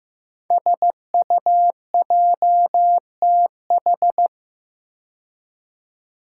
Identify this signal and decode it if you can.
Morse code, heard as a keyed tone, about 15 wpm, 700 Hz. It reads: SUJTH
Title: Code Groups - 5 Letter:Numbers - 15wpm